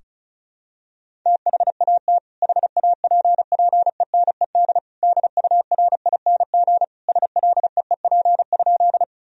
Morse code, heard as a keyed tone, about 34 wpm, 700 Hz. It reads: THAT HAPPENED DURING SLEEP?
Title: Sentences from Top 1000 Words - Review Part 2 - 35wpm